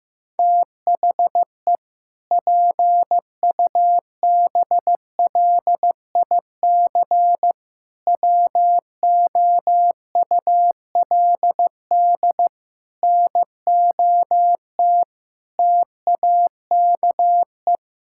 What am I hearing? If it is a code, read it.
Morse code, heard as a keyed tone, about 15 wpm, 700 Hz. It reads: THE PUBLIC WOULD NOT TAKE
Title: Sentences from Top 1600 Words 15wpm